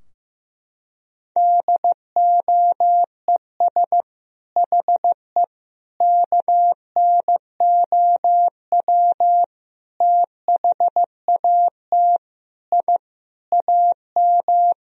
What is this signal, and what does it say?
Morse code, heard as a keyed tone, about 16 wpm, 700 Hz. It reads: DOES HE KNOW THAT I AM
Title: Sentences from Top 1000 Words - Review Part 4 - 15wpm